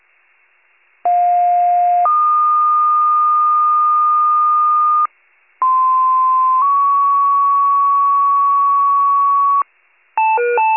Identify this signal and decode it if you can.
Two-tone page, heard as a one-second tone, then a three-second tone, about 710 then 1220 Hz, about 1040 then 1120 Hz. It is heard from 1.0 to 9.6 s.